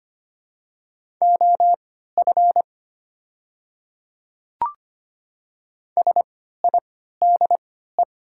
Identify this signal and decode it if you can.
Morse code, heard as a keyed tone, about 25 wpm, spaced as for 14 wpm, 690 Hz. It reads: OF SIDE
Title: 3x Character Spacing - Top 100 Words - 25wpm